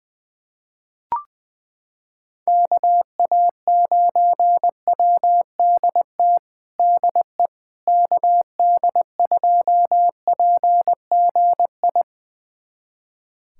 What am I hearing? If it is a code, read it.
Morse code, heard as a keyed tone, about 21 wpm, 700 Hz. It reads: KA9WDT DE KD2PGI
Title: USA Callsign DE Callsign - 6 Characters - 20wpm